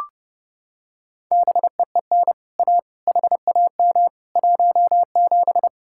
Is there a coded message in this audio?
Yes: Morse code from 1.3 to 5.7 s, heard as a keyed tone, about 29 wpm, 700 Hz. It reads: BEEN A HAM 17